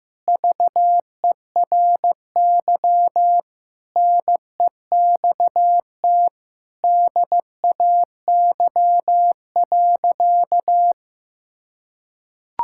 Morse code, heard as a keyed tone, about 15 wpm, 700 Hz. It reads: VERY NEXT DAY.
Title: Sentences from Top 1000 Words - Review Part 6 - 15wpm